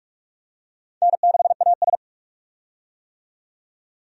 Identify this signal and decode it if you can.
Morse code, heard as a keyed tone, about 46 wpm, 700 Hz. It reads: N6AS
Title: USA Call Signs - 4 Characters - 45wpm